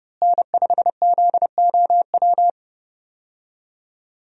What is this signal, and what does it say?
Morse code, heard as a keyed tone, about 30 wpm, 700 Hz. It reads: N5ZOW
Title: USA Call Signs - 5 Characters - Repeated 2x - 30wpm